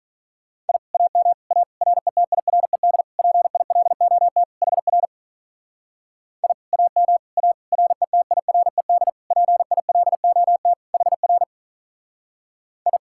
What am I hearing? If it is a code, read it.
Morse code, heard as a keyed tone, about 41 wpm, 700 Hz. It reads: I AM A RETIRED PILOT HR I AM A RETIRED PILOT HR I
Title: Speed-Racing Common QSO Phrases Encore - 35wpm